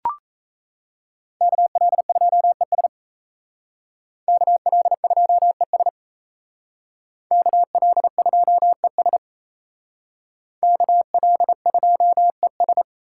Morse code, heard as a keyed tone, about 36 wpm, 700 Hz. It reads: KL2ES KL2ES KL2ES KL2ES